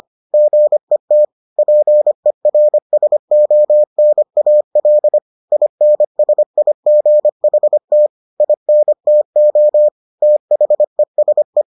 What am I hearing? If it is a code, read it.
Morse code, heard as a keyed tone, about 25 wpm, 600 Hz. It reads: GET PERSONAL INSIGHT INTO THESE